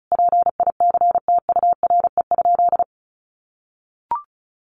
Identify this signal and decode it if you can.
Morse code, heard as a keyed tone, about 34 wpm, 700 Hz. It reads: PICTURE?
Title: A Sound Of Thunder 35wpm